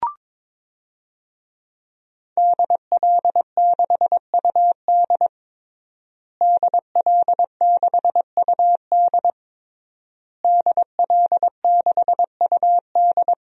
Morse code, heard as a keyed tone, about 22 wpm, 700 Hz. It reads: DL6UD DL6UD DL6UD
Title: DXCC-100 Call Signs - 5 Characters - Repeated 3x - 22wpm